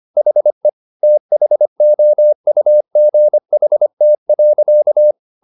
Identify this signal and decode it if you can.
Morse code, heard as a keyed tone, about 25 wpm, 600 Hz. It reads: HE THOUGHT.